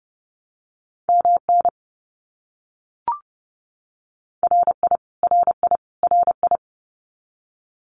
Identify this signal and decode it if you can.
Morse code, heard as a keyed tone, about 30 wpm, 700 Hz. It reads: MN RI RI RI
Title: US States Repeated 3x 30wpm